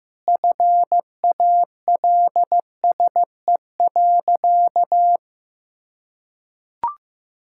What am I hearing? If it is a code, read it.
Morse code, heard as a keyed tone, about 14 wpm, 700 Hz. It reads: FALSE.
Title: Sentences from Top 1600 Words 15wpm